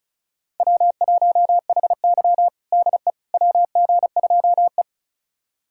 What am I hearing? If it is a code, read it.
Morse code, heard as a keyed tone, about 36 wpm, 700 Hz. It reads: W1HY DE WG2E
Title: USA Callsign DE Callsign - 4 Characters - 35wpm